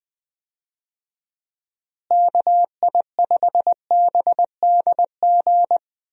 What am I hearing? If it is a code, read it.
Morse code, heard as a keyed tone, about 20 wpm, 700 Hz. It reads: KI5BDG
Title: USA Call Signs - 6 Characters - Repeated 2x - 20wpm